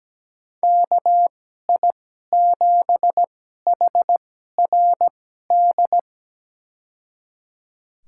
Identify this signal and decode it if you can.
Morse code, heard as a keyed tone, about 17 wpm, spaced as for 12 wpm, 700 Hz. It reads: KI7HRD
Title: USA Call Signs - 2x Character Spacing - 17wpm